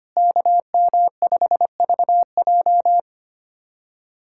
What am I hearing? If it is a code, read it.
Morse code, heard as a keyed tone, about 25 wpm, 700 Hz. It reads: KM5VJ
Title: USA Callsign DE Callsign - 5 Characters - 25wpm